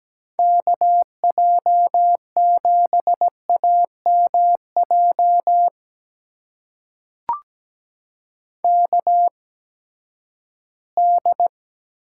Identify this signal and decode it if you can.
Morse code, heard as a keyed tone, about 17 wpm, spaced as for 12 wpm, 700 Hz. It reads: KJ7AMJ KD
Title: USA Call Signs - 6 Characters - 8x Character Spacing - 17wpm